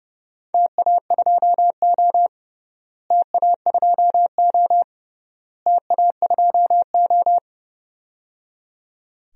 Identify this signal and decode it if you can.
Morse code, heard as a keyed tone, about 30 wpm, 700 Hz. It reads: TA2O TA2O TA2O
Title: DXCC-100 Call Signs - 4 Characters - Repeated 3x - 30wpm